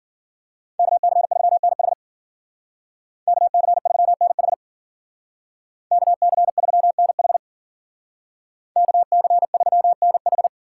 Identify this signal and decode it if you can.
Morse code, heard as a keyed tone, about 52 wpm, 700 Hz. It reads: KC3NH KC3NH KC3NH KC3NH